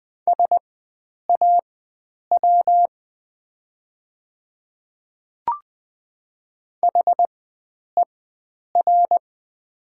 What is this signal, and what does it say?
Morse code, heard as a keyed tone, about 20 wpm, spaced as for 9 wpm, 700 Hz. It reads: SAW HER
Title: 4x Character Spacing - Top 500 Words - 20wpm